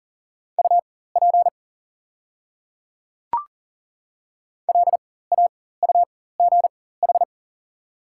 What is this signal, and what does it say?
Morse code, heard as a keyed tone, about 40 wpm, spaced as for 19 wpm, 690 Hz. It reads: UP LAUGH
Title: 4x Character Spacing - Top 500 Words - 40wpm